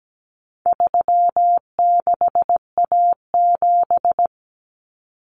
Morse code, heard as a keyed tone, about 17 wpm, 700 Hz. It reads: 36A7
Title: Common Cut Numbers - 4-Digit 17wpm